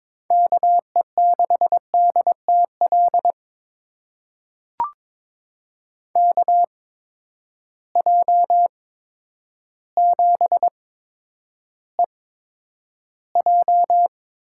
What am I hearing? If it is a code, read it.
Morse code, heard as a keyed tone, about 22 wpm, spaced as for 10 wpm, 700 Hz. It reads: KE6DTL KJ7EJ
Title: USA Call Signs - 8x Character Spacing - 22wpm